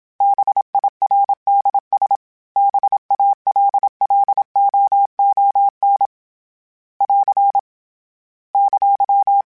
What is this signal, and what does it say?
Morse code, heard as a keyed tone, about 27 wpm, 800 Hz. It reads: BIRDS BALLOON <AR> !